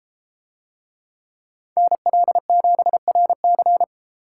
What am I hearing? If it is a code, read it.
Morse code, heard as a keyed tone, about 32 wpm, 700 Hz. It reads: NL7RC